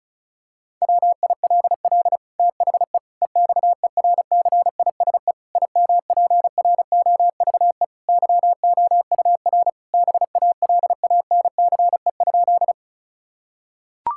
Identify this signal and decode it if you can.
Morse code, heard as a keyed tone, about 36 wpm, 700 Hz. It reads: WILL THE EXERCISE IMPROVE YOUR BALANCE?
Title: Sentences from Top 1000 Words - Review Part 6 - 35wpm